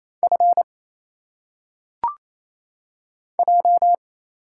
Morse code, heard as a keyed tone, about 27 wpm, 700 Hz. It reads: F J
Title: Single Letters - Phonetic Pronunciation 28wpm